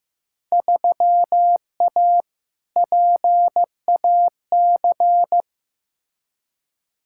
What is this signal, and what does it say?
Morse code, heard as a keyed tone, about 15 wpm, 700 Hz. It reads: 3A PAC
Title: Field Day Exchanges Encore - 15wpm